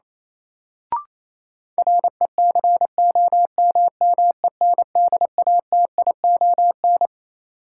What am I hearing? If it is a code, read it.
Morse code, heard as a keyed tone, about 28 wpm, 700 Hz. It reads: RECOMMENDATION
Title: Sentences from Top 1400 Words - New Word Drill Alt - 28wpm